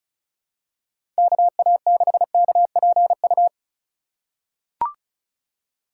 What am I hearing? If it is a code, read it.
Morse code, heard as a keyed tone, about 36 wpm, 700 Hz. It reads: KA6KPU
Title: USA Call Signs - 6 Characters - 4x Character Spacing - 35wpm